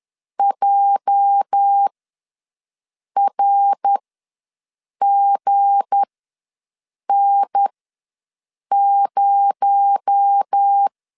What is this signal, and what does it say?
Morse code, heard as a keyed tone, about 10 wpm, spaced as for 6 wpm, 800 Hz. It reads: JRGN0
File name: LEZIONI / 8 YJ45 / 8 YJ ws 10wpm 4043